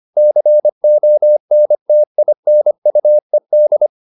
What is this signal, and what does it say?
Morse code, heard as a keyed tone, about 25 wpm, 600 Hz. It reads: CONTINUED
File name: the old man and the sea part 1 25wpm